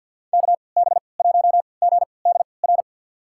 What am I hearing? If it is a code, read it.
Morse code, heard as a keyed tone, about 52 wpm, spaced as for 29 wpm, 700 Hz. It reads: KB1GDR
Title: USA Call Signs - 6 Characters - 3x Character Spacing - 50wpm